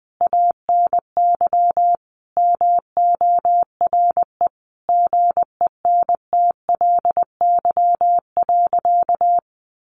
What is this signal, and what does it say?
Morse code, heard as a keyed tone, about 20 wpm, 700 Hz. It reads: ANY MORE GENTLY.
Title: Sentences from Top 1500 Words 20wpm